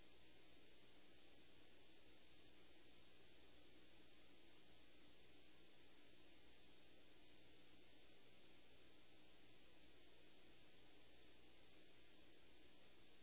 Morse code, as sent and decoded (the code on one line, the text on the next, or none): none